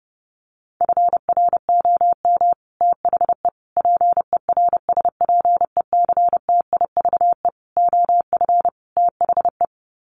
Morse code, heard as a keyed tone, about 30 wpm, 700 Hz. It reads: FROM THE PERSPECTIVE OF THE
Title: Sets of 5 Words Encore2 30wpm